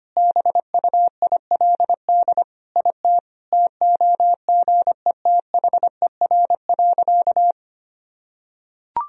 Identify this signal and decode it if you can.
Morse code, heard as a keyed tone, about 25 wpm, 700 Hz. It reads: BUILD IT TOGETHER.
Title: Sentences from Top 1000 Words - Review Part 1 - 25wpm